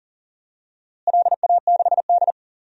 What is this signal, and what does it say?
Morse code, heard as a keyed tone, about 40 wpm, 700 Hz. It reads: LA6D
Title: DXCC-100 Call Signs - 4 Characters - Repeated 3x - 40wpm